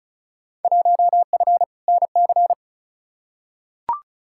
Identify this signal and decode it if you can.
Morse code, heard as a keyed tone, about 36 wpm, 700 Hz. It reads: 1F NC
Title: Field Day Exchanges - 35wpm